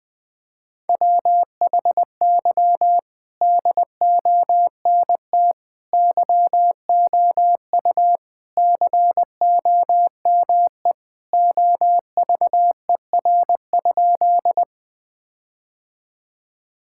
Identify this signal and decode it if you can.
Morse code, heard as a keyed tone, about 20 wpm, 700 Hz. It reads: WHY DONT YOU COME OVER?
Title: Sentences with 4-Word Phrases - Spoken Beginning - 20wpm